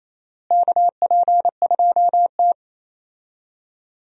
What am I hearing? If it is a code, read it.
Morse code, heard as a keyed tone, about 28 wpm, 700 Hz. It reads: KP2T
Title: USA Call Signs - 4 Characters - Repeated 3x - 28wpm